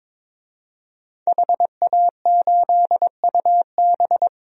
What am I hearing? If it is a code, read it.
Morse code, heard as a keyed tone, about 22 wpm, 700 Hz. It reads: HA8UB
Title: DXCC-100 Call Signs - 5 Characters - Repeated 3x - 22wpm